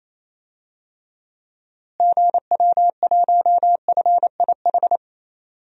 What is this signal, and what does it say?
Morse code, heard as a keyed tone, about 28 wpm, 700 Hz. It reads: GW1FIH
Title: DXCC-100 Call Signs - 6 Characters - Repeated 3x - 28wpm